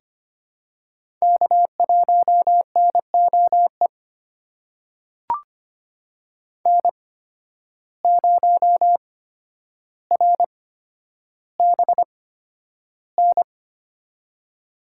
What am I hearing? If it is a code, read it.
Morse code, heard as a keyed tone, about 25 wpm, spaced as for 11 wpm, 700 Hz. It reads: K1NOE N0RBN
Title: USA Call Signs - 5 Characters - 8x Character Spacing - 25wpm